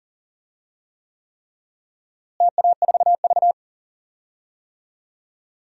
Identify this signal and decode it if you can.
Morse code, heard as a keyed tone, about 40 wpm, 700 Hz. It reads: TA4V
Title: DXCC-100 Call Signs - 4 Characters - Repeated 3x - 40wpm